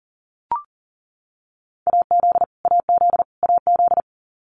Morse code, heard as a keyed tone, about 40 wpm, 700 Hz. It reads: AZ AZ AZ
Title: US States Repeated 3x 40wpm